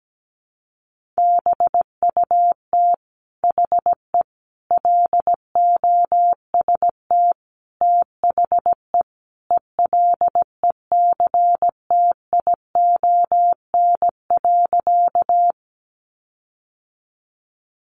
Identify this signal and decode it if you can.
Morse code, heard as a keyed tone, about 17 wpm, 700 Hz. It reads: BUT HE LOST THE ELECTION.
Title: Sentences from Top 1500 Words 17wpm